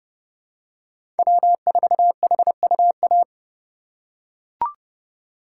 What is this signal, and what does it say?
Morse code, heard as a keyed tone, about 30 wpm, 700 Hz. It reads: W4HUA
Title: USA Call Signs - 5 Characters - 2x Character Spacing - 30wpm